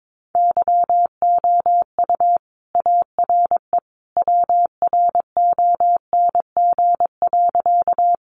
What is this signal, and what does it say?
Morse code, heard as a keyed tone, about 22 wpm, 700 Hz. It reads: YOU ARE WRONG.
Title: Sentences from Top 900 Words 22wpm